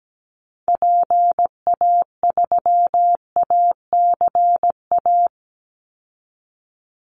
Morse code, heard as a keyed tone, about 17 wpm, 700 Hz. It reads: PA3ACA
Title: Call Signs - 17wpm